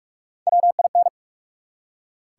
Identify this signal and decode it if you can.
Morse code, heard as a keyed tone, about 44 wpm, 700 Hz. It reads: WIN
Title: Three Letter Words - 45wpm